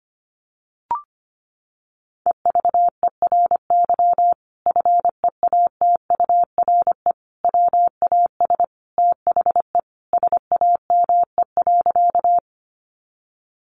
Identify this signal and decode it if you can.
Morse code, heard as a keyed tone, about 25 wpm, 700 Hz. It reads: EVERY FEATURE WAS THE SAME.
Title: Sentences from Top 800 Words 25wpm